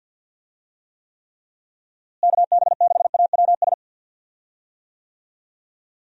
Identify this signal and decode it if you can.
Morse code, heard as a keyed tone, about 46 wpm, 700 Hz. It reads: KB6AWS
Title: USA Call Signs - 6 Characters - Repeated 3x - 50wpm